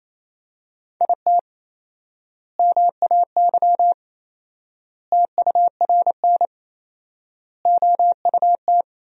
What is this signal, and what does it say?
Morse code, heard as a keyed tone, about 28 wpm, 700 Hz. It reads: IT MAY TURN OUT
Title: 4x Word-Spacing - Sentences from Top 400 Words - 28wpm